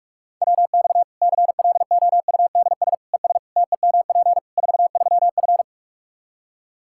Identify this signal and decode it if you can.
Morse code, heard as a keyed tone, about 46 wpm, 700 Hz. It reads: WX CLOUDS ES TEMP 43F
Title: Practice QSO - Exchange 2 - Info - 45wpm